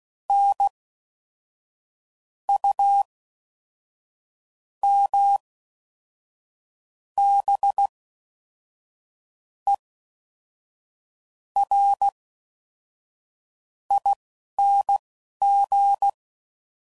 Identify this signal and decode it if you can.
Morse code, heard as a keyed tone, about 16 wpm, spaced as for 11 wpm, 800 Hz. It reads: N U M B E R ING
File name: Suffixes / Special INGs / Fast INGs